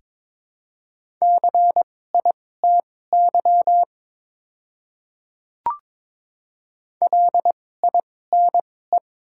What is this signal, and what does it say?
Morse code, heard as a keyed tone, about 21 wpm, spaced as for 16 wpm, 700 Hz. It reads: CITY LINE
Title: 2x Character Spacing - Top 300 Words - 22wpm